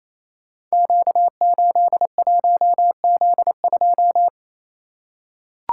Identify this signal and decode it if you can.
Morse code, heard as a keyed tone, about 27 wpm, 700 Hz. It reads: Q81Z2
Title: Code Groups - 5 Letter:Numbers - 28wpm